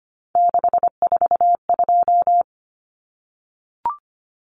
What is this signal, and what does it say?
Morse code, heard as a keyed tone, about 25 wpm, 700 Hz. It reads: 642
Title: Numbers - 3-Digit 25wpm